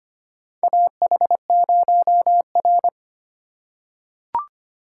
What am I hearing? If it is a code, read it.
Morse code, heard as a keyed tone, about 25 wpm, 700 Hz. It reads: AH0R
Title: USA Call Signs - 4 Characters - 3x Character Spacing - 25wpm